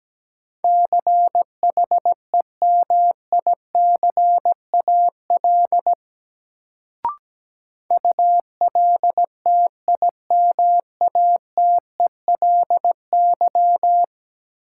Morse code, heard as a keyed tone, about 17 wpm, 700 Hz. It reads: CHEMICAL ULTIMATELY
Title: Sentences from Top 1200 Words - New Word Drill - 17wpm